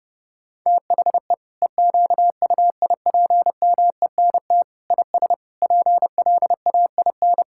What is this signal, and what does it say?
Morse code, heard as a keyed tone, about 30 wpm, 700 Hz. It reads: THE EQUIPMENT IS PLAIN
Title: Sentences from Top 1000 Words - Review Part 5 - 30wpm